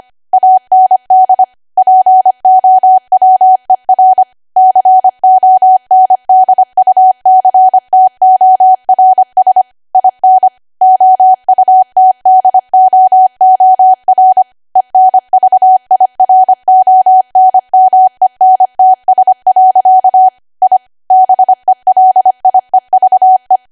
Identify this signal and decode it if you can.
Morse code, heard as a keyed tone, about 25 wpm, 750 Hz. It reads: AND POWER CONDUCTORS IN OUTDOOR ENVIRONMENTS. I BELIEVE
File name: Archive / 25 WPM / 25WPM